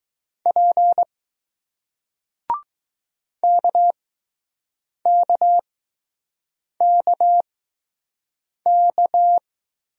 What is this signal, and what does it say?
Morse code, heard as a keyed tone, about 20 wpm, spaced as for 6 wpm, 700 Hz. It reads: P KKKK